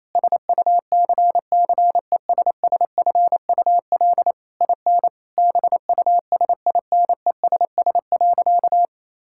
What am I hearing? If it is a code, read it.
Morse code, heard as a keyed tone, about 28 wpm, 700 Hz. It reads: SUCCESSFUL IN BUSINESS.
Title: Sentences from Top 1000 Words - Review Part 6 - 28wpm